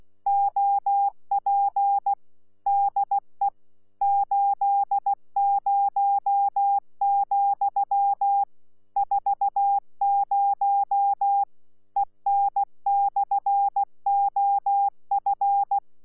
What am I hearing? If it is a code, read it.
Morse code, heard as a keyed tone, about 16 wpm, 800 Hz. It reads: OP DE 80, 40 EN/OF
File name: Z PI4VRZ / PI4VRZ 14-jun-2025